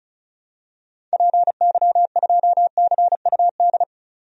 Morse code, heard as a keyed tone, about 36 wpm, 700 Hz. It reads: PY2CUD
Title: DXCC-100 Call Signs - 6 Characters - 35wpm